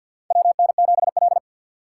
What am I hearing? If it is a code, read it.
Morse code, heard as a keyed tone, about 46 wpm, 700 Hz. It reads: WN7L
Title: USA Call Signs - 4 Characters - 6x Character Spacing - 50wpm